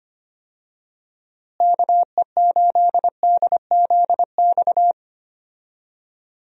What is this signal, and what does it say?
Morse code, heard as a keyed tone, about 25 wpm, 700 Hz. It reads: KE8DZX
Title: USA Call Signs - 6 Characters - Repeated 3x - 25wpm